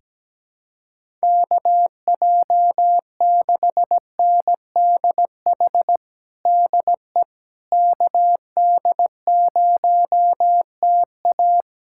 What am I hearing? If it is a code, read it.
Morse code, heard as a keyed tone, about 17 wpm, 700 Hz. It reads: KJ6NDH DE KD0TA
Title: USA Callsign DE Callsign 17wpm